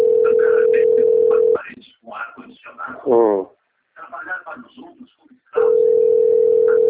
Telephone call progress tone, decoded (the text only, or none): ringback tone